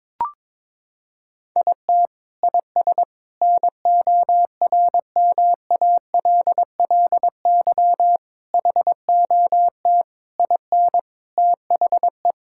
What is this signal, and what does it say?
Morse code, heard as a keyed tone, about 21 wpm, 700 Hz. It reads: IT IS NORMALLY HOT IN THE
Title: Sentences from Top 1000 Words - Review Part 6 - 22wpm